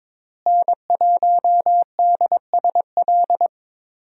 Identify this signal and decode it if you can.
Morse code, heard as a keyed tone, about 22 wpm, 700 Hz. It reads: N1DSL
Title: USA Call Signs - 5 Characters - Repeated 2x - 22wpm